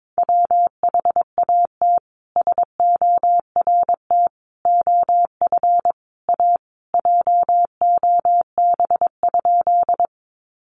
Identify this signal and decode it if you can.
Morse code, heard as a keyed tone, about 22 wpm, 700 Hz. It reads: WHAT SORT OF A JOB?
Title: Sentences from Top 1300 Words 22wpm